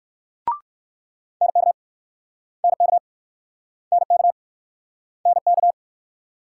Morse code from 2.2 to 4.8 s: NK NK

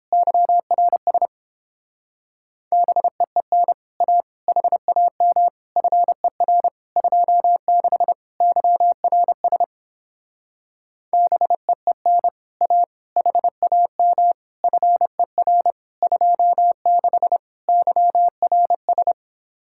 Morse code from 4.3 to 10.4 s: HAM FER 26 YRS

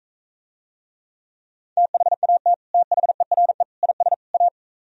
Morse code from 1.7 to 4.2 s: THAT THERE IS

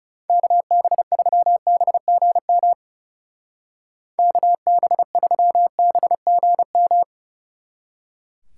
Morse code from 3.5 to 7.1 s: KB3BGM

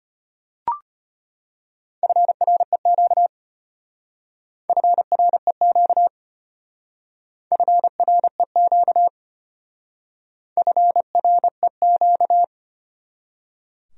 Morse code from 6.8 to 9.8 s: FREQ

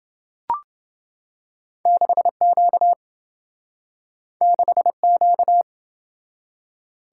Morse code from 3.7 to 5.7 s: 6Q

